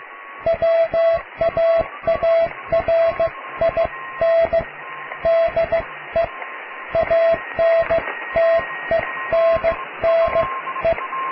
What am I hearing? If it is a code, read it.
Morse code, heard as a keyed tone, about 15 wpm, spaced as for 13 wpm, 660 Hz, with radio noise. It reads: WAARIN DE ANTENNE
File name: cw / tekst(a)+QRM